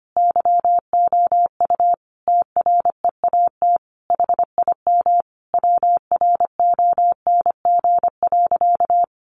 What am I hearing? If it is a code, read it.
Morse code, heard as a keyed tone, about 25 wpm, 700 Hz. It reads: YOU TREAT HIM WRONG.
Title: Sentences from Top 1300 Words 25wpm